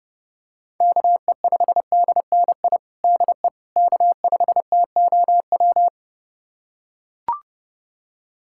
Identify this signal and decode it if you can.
Morse code, heard as a keyed tone, about 32 wpm, 700 Hz. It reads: KE5DNI DE K5TOW
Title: USA Callsign DE Callsign 30wpm